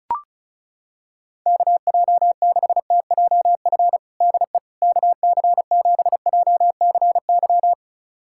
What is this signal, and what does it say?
Morse code, heard as a keyed tone, about 34 wpm, 700 Hz. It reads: KJ6TJF DE KC7JCY
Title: USA Callsign DE Callsign - 6 Characters - 35wpm